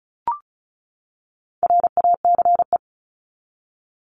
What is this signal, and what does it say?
Morse code, heard as a keyed tone, about 34 wpm, 700 Hz. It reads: RACE
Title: Top 1501-1750 Words 35wpm